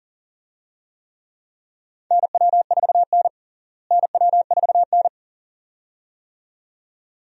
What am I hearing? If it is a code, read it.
Morse code, heard as a keyed tone, about 37 wpm, 700 Hz. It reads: NW4N NW4N
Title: USA Call Signs - 4 Characters - Repeated 2x - 40wpm